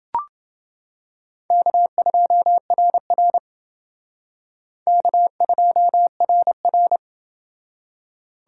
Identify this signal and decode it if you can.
Morse code, heard as a keyed tone, about 29 wpm, 700 Hz. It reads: K2RR K2RR